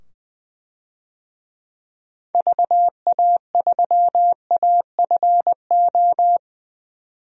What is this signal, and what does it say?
Morse code, heard as a keyed tone, about 20 wpm, 700 Hz. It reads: VA3AFO
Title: DXCC-100 Call Signs - 6 Characters - 20wpm